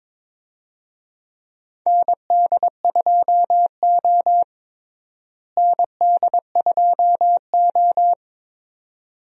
Morse code, heard as a keyed tone, about 21 wpm, 700 Hz. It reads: ND2O ND2O
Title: USA Call Signs - 4 Characters - Repeated 2x - 22wpm